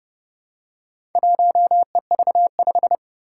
Morse code, heard as a keyed tone, about 28 wpm, 700 Hz. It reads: 1EV5
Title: Four Challenging Letter-Number Combinations 30wpm